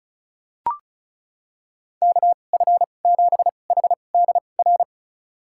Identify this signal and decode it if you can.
Morse code, heard as a keyed tone, about 34 wpm, spaced as for 25 wpm, 700 Hz. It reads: KF7HDR
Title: USA Call Signs - 6 Characters - 2x Character Spacing - 35wpm